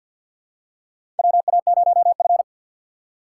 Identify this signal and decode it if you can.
Morse code, heard as a keyed tone, about 51 wpm, 700 Hz. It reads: WA0F
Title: USA Call Signs - 3x Character Spacing - 50wpm